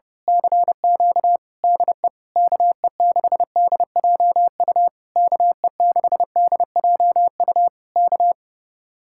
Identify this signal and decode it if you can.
Morse code, heard as a keyed tone, about 28 wpm, 700 Hz. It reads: CQ DE KE6DJU KE6DJU K
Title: Speed-Racing Common QSO Phrases - 30wpm